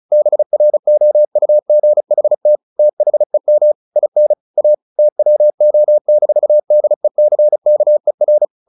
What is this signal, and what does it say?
Morse code, heard as a keyed tone, about 34 wpm, 600 Hz. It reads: BROUGHT THEM IN A TWO-DECKER
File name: the old man and the sea / the old man and the sea 1 35wpm